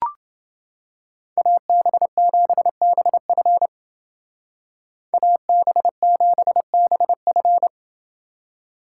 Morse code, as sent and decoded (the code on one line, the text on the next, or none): .- -... --... -... ..-. / .- -... --... -... ..-.
AB7BF AB7BF